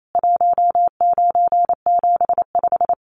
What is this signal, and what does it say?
Morse code, heard as a keyed tone, about 28 wpm, 700 Hz. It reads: 1975
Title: Common QSO Phrases 28wpm